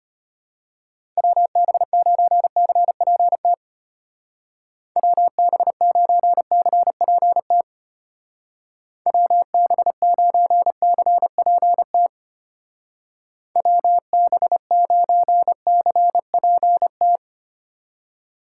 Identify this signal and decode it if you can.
Morse code, heard as a keyed tone, about 31 wpm, 700 Hz. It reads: WB9CPT WB9CPT WB9CPT WB9CPT
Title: Speed Racing - USA 2x3 Call Signs 25wpm